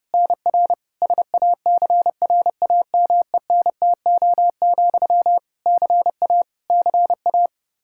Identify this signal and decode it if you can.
Morse code, heard as a keyed tone, about 30 wpm, 700 Hz. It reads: NR SACRAMENTO, CA CA
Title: K1USN SST QSO Practice 30wpm